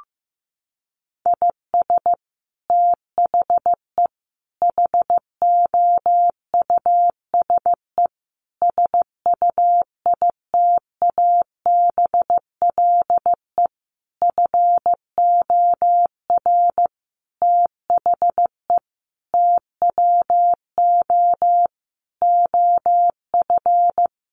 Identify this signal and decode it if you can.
Morse code, heard as a keyed tone, about 14 wpm, 700 Hz. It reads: IS THE HOUSE SUITABLE FOR THE TWO OF
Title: Sentences from Top 1400 Words 15wpm